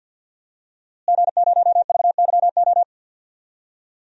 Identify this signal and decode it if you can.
Morse code, heard as a keyed tone, about 52 wpm, 700 Hz. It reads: G0VYO
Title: DXCC-100 Call Signs - 5 Characters - Repeated 3x - 50wpm